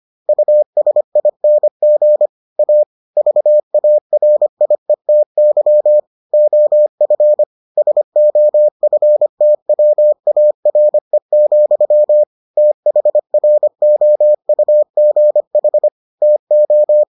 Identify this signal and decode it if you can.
Morse code, heard as a keyed tone, about 25 wpm, 600 Hz. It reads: USING A VARIETY OF SOFTWARE, THROUGH TO